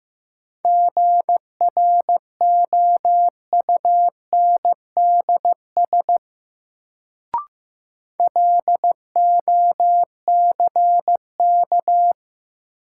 Morse code, heard as a keyed tone, about 15 wpm, 700 Hz. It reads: GROUNDS LOCK
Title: Sentences from Top 1300 Words - New Word Drill - 15wpm